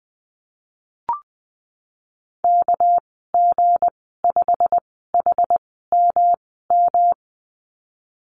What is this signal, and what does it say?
Morse code, heard as a keyed tone, about 20 wpm, spaced as for 15 wpm, 700 Hz. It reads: KG5HMM
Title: 207 Lesson 33g - Letter j - Call Signs 20wpm